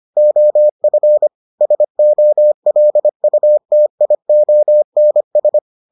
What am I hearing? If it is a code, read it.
Morse code, heard as a keyed tone, about 25 wpm, 600 Hz. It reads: OF SOLUTIONS